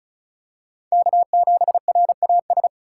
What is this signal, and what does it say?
Morse code, heard as a keyed tone, about 36 wpm, 700 Hz. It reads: K7RAS
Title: USA Call Signs - 3x Character Spacing - 35wpm